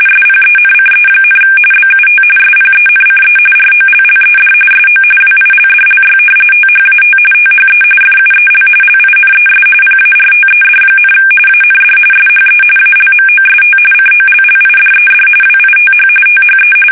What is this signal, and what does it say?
RTTY, about 45.45 baud. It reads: SPENCE, BUT SPENCE ROLLED SO WILDLY THAT THE FUEL LINES PARTED, AND SHE ALMOST COLLIDED WITH THE
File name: fsk850 / battleshipnj